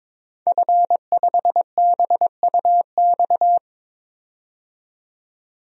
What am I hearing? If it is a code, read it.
Morse code, heard as a keyed tone, about 22 wpm, 700 Hz. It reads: F5BUX
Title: DXCC-100 Call Signs - 5 Characters - 22wpm